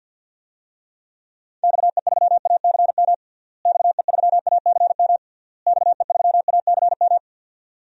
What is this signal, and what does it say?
Morse code, heard as a keyed tone, about 52 wpm, 700 Hz. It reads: XE3ACM XE3ACM XE3ACM
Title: DXCC-100 Call Signs - 6 Characters - Repeated 3x - 50wpm